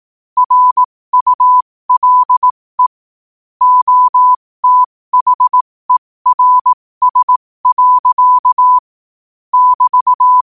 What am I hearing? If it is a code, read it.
Morse code, heard as a keyed tone, about 18 wpm, spaced as for 15 wpm, 1000 Hz. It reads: RULE OTHERS. <BT>